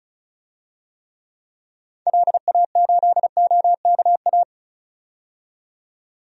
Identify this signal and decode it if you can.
Morse code, heard as a keyed tone, about 34 wpm, 700 Hz. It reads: LA8OKA
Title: DXCC-100 Call Signs - 6 Characters - 35wpm